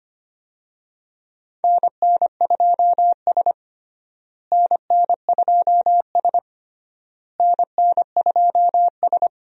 Morse code, heard as a keyed tone, about 25 wpm, 700 Hz. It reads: NN2S NN2S NN2S
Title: USA Call Signs - 4 Characters - Repeated 3x - 25wpm